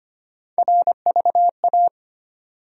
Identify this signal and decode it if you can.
Morse code, heard as a keyed tone, about 25 wpm, 700 Hz. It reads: RVA
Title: USA Call Sign Postfixes - 3 Letters 25wpm